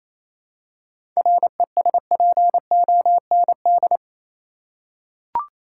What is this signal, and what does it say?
Morse code, heard as a keyed tone, about 28 wpm, 700 Hz. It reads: RESPOND
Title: Seven Letter Words - 28wpm